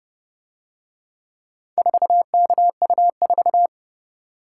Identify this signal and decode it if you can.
Morse code, heard as a keyed tone, about 29 wpm, 700 Hz. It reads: 4KU4